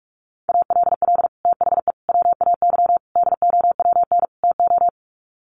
Morse code, heard as a keyed tone, about 44 wpm, 700 Hz. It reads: ALL THE WAY DOWN TO
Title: Sets of 5 Words Encore2 45wpm